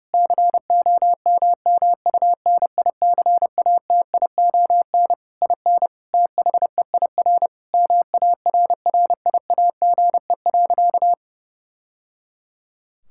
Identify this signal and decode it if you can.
Morse code, heard as a keyed tone, about 30 wpm, 700 Hz. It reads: COMMUNICATION IN THEIR MARRIAGE.
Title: Sentences from Top 1000 Words - Review Part 6 - 30wpm